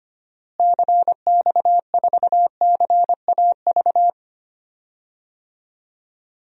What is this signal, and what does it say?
Morse code, heard as a keyed tone, about 25 wpm, 700 Hz. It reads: CX4CAV
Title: DXCC-100 Call Signs - 6 Characters - Repeated 3x - 25wpm